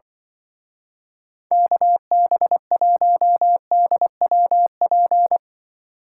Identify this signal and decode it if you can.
Morse code, heard as a keyed tone, about 23 wpm, 700 Hz. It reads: KB1DWP
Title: Speed Racing - USA Call Signs 20wpm